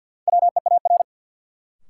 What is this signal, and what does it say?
Morse code, heard as a keyed tone, about 52 wpm, 700 Hz. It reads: WEAR
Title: Sentences from Top 1000 Words - Review Part 6 - New Word Drill - 50wpm